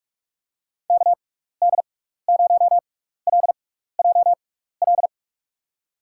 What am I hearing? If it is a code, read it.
Morse code, heard as a keyed tone, about 46 wpm, spaced as for 15 wpm, 700 Hz. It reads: KD0LJL
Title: USA Call Signs - 6 Characters - 6x Character Spacing - 45wpm